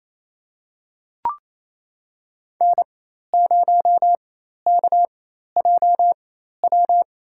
Morse code, heard as a keyed tone, about 28 wpm, spaced as for 13 wpm, 700 Hz. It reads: N0KJW
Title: USA Call Signs - 4x Character Spacing - 28wpm